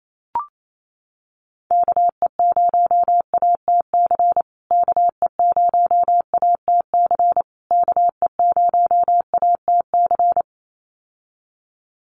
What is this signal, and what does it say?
Morse code, heard as a keyed tone, about 28 wpm, 700 Hz. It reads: KE0ATC KE0ATC KE0ATC
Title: USA Call Signs Repeated 3x 28wpm